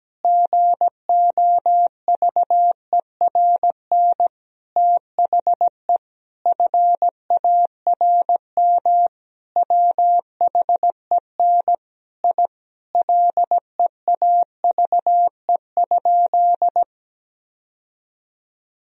Morse code, heard as a keyed tone, about 17 wpm, 700 Hz. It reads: GOVERN THE FARM WHEN I LEAVE?
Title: Sentences from Top 1000 Words - Review Part 2 - 17wpm